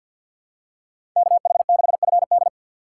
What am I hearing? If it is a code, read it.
Morse code, heard as a keyed tone, about 53 wpm, 700 Hz. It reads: KH6FD
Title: USA Call Signs - 5 Characters - 4x Character Spacing - 50wpm